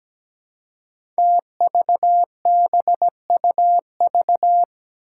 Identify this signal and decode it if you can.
Morse code, heard as a keyed tone, about 17 wpm, 700 Hz. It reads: TVBUV
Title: Five Challenging Letter-Number Combinations 17wpm